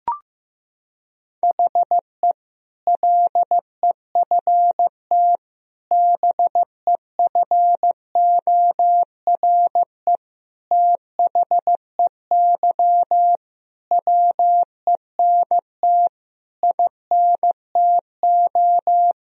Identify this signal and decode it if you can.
Morse code, heard as a keyed tone, about 15 wpm, 700 Hz. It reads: HE LEFT BEFORE THEY WENT INTO
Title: Sentences from Top 1000 Words - Review Part 4 - 15wpm